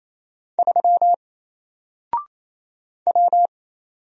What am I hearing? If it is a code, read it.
Morse code, heard as a keyed tone, about 27 wpm, 700 Hz. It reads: 3 W
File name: Single-Letter-Number-28wpm